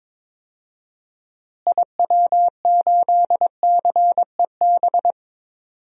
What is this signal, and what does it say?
Morse code, heard as a keyed tone, about 22 wpm, 700 Hz. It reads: IW8CEB
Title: DXCC-100 Call Signs - 6 Characters - Repeated 2x - 22wpm